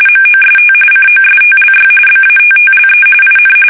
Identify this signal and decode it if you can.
RTTY, about 45.45 baud. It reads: MOST COSTLY BATTLE IN
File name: fsk850 / battleshipnj